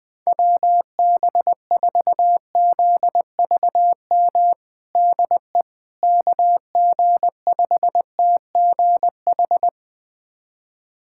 Morse code, heard as a keyed tone, about 21 wpm, 700 Hz. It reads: WB4ZVM DE KG5TGH
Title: USA Callsign DE Callsign - 6 Characters - 20wpm